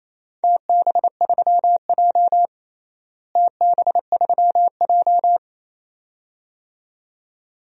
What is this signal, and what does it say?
Morse code, heard as a keyed tone, about 28 wpm, 700 Hz. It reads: TB3J TB3J
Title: DXCC-100 Call Signs - 4 Characters - Repeated 2x - 28wpm